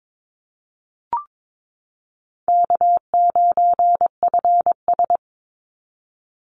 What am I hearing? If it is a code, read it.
Morse code, heard as a keyed tone, about 22 wpm, 700 Hz. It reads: K9FS
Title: Easy Call Signs - 22wpm